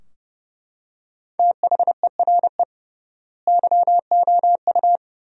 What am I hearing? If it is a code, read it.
Morse code, heard as a keyed tone, about 30 wpm, 700 Hz. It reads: THERE YOU
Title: 3x Word-Spacing - Sentences from Top 100 Words - 30wpm